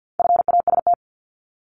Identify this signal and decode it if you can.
Morse code, heard as a keyed tone, about 50 wpm, 700 Hz. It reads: FAST
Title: Top 500 Words 50wpm